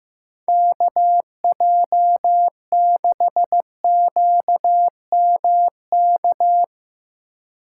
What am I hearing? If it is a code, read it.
Morse code, heard as a keyed tone, about 16 wpm, 700 Hz. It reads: KJ6QMK
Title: Speed Racing - USA 2x3 Call Signs 15wpm